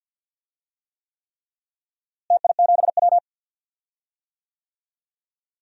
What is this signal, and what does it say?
Morse code, heard as a keyed tone, about 52 wpm, 700 Hz. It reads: TI7W
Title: DXCC-100 Call Signs - 4 Characters - Repeated 3x - 50wpm